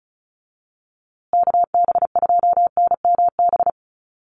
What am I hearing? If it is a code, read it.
Morse code, heard as a keyed tone, about 34 wpm, 700 Hz. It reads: KB2NMB
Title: USA Call Signs Repeated 2x 35wpm